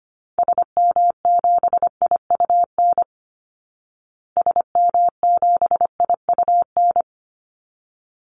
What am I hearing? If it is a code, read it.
Morse code, heard as a keyed tone, about 25 wpm, 700 Hz. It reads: SM7IUN SM7IUN
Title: Call Signs Repeated 3x - 25wpm